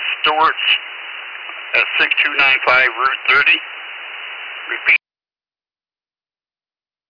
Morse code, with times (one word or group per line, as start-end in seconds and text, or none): none